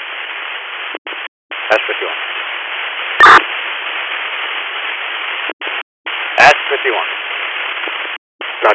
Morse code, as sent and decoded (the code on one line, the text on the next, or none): none